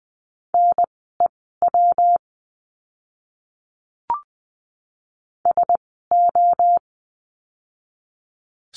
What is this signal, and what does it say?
Morse code, heard as a keyed tone, about 20 wpm, spaced as for 14 wpm, 700 Hz. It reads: NEW SO
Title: 230 Lesson 37f - Letter x - Words 20wpm